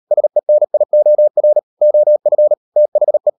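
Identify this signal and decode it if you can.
Morse code, heard as a keyed tone, about 38 wpm, 600 Hz. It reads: SENIOR OF THE
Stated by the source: Morse code practice at 35–39 wpm sending English text of Ken Follett's novel Never, from CW med Gnister.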